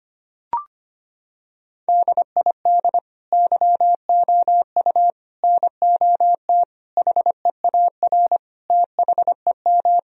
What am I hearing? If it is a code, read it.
Morse code, heard as a keyed tone, about 25 wpm, 700 Hz. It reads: DID YOU NOT HEAR THEM
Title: Sentences from Top 1000 Words - Review Part 2 - 25wpm